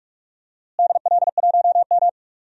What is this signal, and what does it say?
Morse code, heard as a keyed tone, about 46 wpm, 700 Hz. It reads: DL1M
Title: DXCC-100 Call Signs - 4 Characters - Repeated 2x - 45wpm